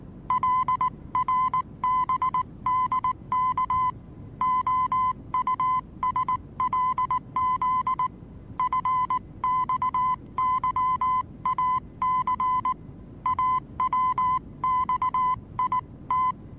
Morse code, heard as a keyed tone, about 19 wpm, 1020 Hz. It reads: LRBDK OUSLZ FXYAC AWXIT
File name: repos / Morsekurs / DJ4UF Morsen26